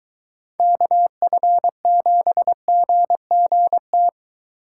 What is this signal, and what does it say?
Morse code, heard as a keyed tone, about 23 wpm, 700 Hz. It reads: KF7GGT